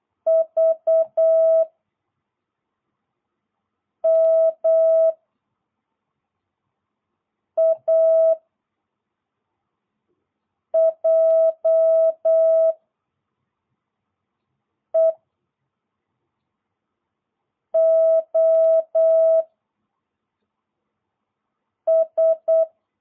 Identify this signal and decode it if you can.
Morse code, heard as a keyed tone, about 8 wpm, 660 Hz. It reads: V M A J E O S